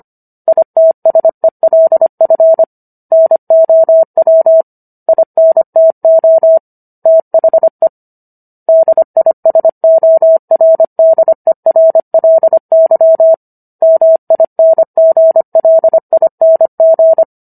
Morse code, heard as a keyed tone, about 25 wpm, 650 Hz. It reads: ITSELF NOW INTO THE DISORDERLY MINGLING